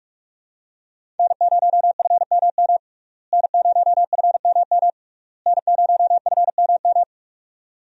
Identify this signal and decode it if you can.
Morse code, heard as a keyed tone, about 42 wpm, 700 Hz. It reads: N0FMM N0FMM N0FMM
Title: USA Call Signs - 5 Characters - Repeated 3x - 45wpm